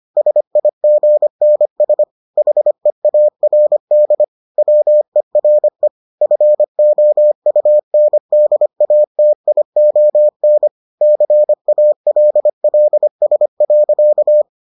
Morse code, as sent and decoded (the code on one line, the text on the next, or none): ... .. --. -. ... / .... .- .-. -.. / .-- . .-. / ..-. --- ..- -. -.. .- - .. --- -. / -.-. .- .-.. .-.. ... .-.-.-
SIGNS HEARD WERE FOUNDATION CALLS.